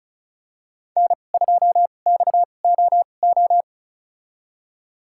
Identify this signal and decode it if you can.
Morse code, heard as a keyed tone, about 36 wpm, spaced as for 25 wpm, 700 Hz. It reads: N2XOO
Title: USA Call Signs - 5 Characters - 2x Character Spacing - 35wpm